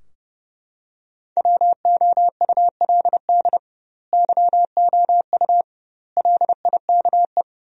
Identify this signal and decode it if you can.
Morse code, heard as a keyed tone, about 30 wpm, 700 Hz. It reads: WOULD YOU LIKE
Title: 2x Word-Spacing - Sentences from Top 100 Words - 30wpm